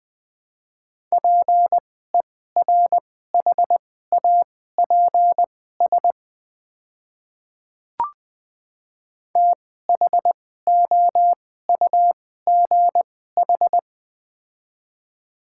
Morse code, heard as a keyed tone, about 20 wpm, spaced as for 14 wpm, 700 Hz. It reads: PERHAPS THOUGH
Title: 2x Character Spacing - Top 500 Words - 20wpm